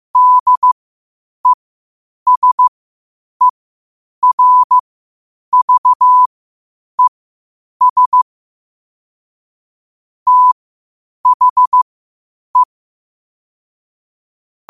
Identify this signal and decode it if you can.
Morse code, heard as a keyed tone, about 15 wpm, spaced as for 8 wpm, 1000 Hz. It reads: DESERVES THE
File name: qotdfiles / 8wpm